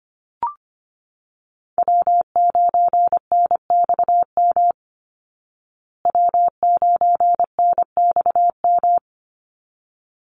Morse code, heard as a keyed tone, about 25 wpm, 700 Hz. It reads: W9NXM W9NXM
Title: Call Signs Repeated 3x - 25wpm